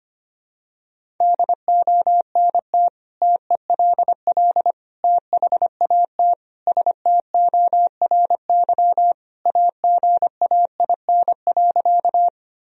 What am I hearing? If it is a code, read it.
Morse code, heard as a keyed tone, about 25 wpm, 700 Hz. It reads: DONT TELL THAT STORY AGAIN.
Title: Sentences from Top 1000 Words - Review Part 1 - 25wpm